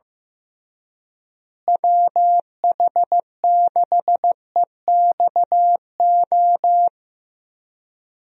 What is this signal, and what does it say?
Morse code, heard as a keyed tone, about 15 wpm, 700 Hz. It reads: WH6EXO
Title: USA Call Signs - 6 Characters - Repeated 3x - 15wpm